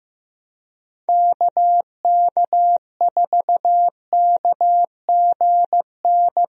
Morse code, heard as a keyed tone, about 15 wpm, 700 Hz. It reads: KK4KGN
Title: USA Call Signs - 6 Characters - 15wpm